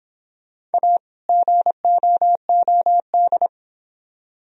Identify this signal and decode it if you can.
Morse code, heard as a keyed tone, about 31 wpm, 700 Hz. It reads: A GOOD